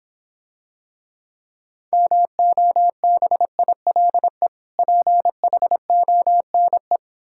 Morse code, heard as a keyed tone, about 26 wpm, 700 Hz. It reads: MOBILE PHONE